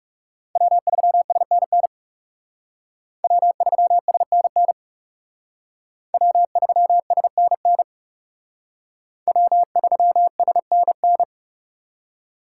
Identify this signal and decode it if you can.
Morse code, heard as a keyed tone, about 37 wpm, 700 Hz. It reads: W3SNN W3SNN W3SNN W3SNN
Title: Speed Racing - USA 1x3 Call Signs 30wpm